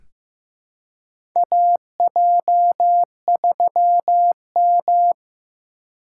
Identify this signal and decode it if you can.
Morse code, heard as a keyed tone, about 15 wpm, 700 Hz. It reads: AJ3M
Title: USA Call Signs - 3x Character Spacing - 15wpm